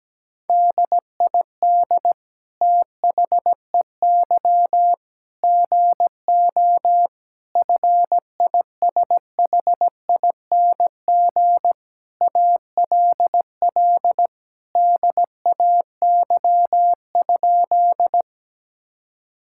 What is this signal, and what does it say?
Morse code, heard as a keyed tone, about 17 wpm, 700 Hz. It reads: DID THEY GO FISHING ALL DAY?
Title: Sentences from Top 1000 Words - Review Part 5 - 17wpm